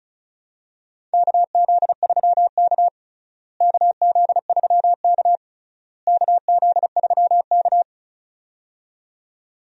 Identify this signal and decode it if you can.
Morse code, heard as a keyed tone, about 36 wpm, 700 Hz. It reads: KZ3K KZ3K KZ3K
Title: USA Call Signs - 4 Characters - Repeated 3x - 35wpm